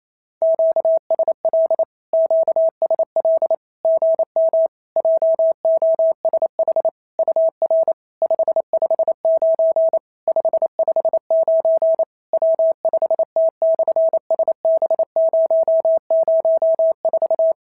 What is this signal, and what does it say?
Morse code, heard as a keyed tone, about 27 wpm, 650 Hz. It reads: QSL QSL GM JOSH UR 559 559 W5T/SB004
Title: SOTA QSO Practice 28wpm